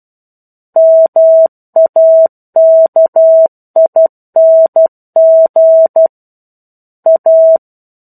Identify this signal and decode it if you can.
Morse code, heard as a keyed tone, about 12 wpm, 650 Hz. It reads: MAKING A
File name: 01-Chapter-01 12 WPM